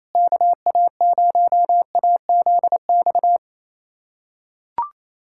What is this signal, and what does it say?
Morse code, heard as a keyed tone, about 27 wpm, 700 Hz. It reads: KA0AZX
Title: USA Callsign DE Callsign 28wpm